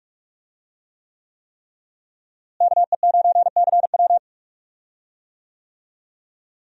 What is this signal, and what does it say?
Morse code, heard as a keyed tone, about 43 wpm, 700 Hz. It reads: KE9CW
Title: USA Call Signs - 5 Characters - Repeated 3x - 45wpm